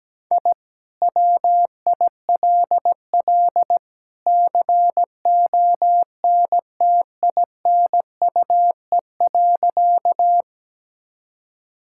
Morse code, heard as a keyed tone, about 17 wpm, 700 Hz. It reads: I WILL CONTINUE.
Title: Sentences from Top 1000 Words - Review Part 6 - 17wpm